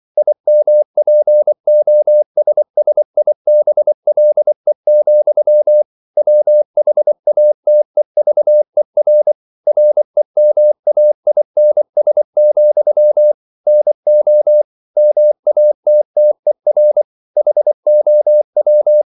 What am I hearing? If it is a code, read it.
Morse code, heard as a keyed tone, about 24 wpm, 600 Hz. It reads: IMPOSSIBLE, WHATEVER REMAINS, NO MATTER HOW
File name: CW / Practice / EnglishQuotations0224